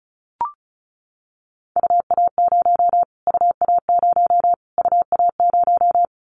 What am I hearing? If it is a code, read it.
Morse code, heard as a keyed tone, about 34 wpm, 700 Hz. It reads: UA0 UA0 UA0
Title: Callsign Prefixes for Top 25 DXCC - Repeated 3x - 35wpm